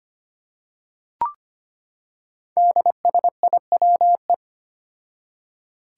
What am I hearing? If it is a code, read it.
Morse code, heard as a keyed tone, about 25 wpm, 700 Hz. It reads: DSIWE
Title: Five Challenging Letter-Number Combinations 25wpm